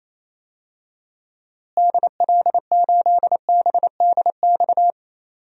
Morse code, heard as a keyed tone, about 28 wpm, 700 Hz. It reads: DL8BDX
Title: DXCC-100 Call Signs - 6 Characters - Repeated 3x - 28wpm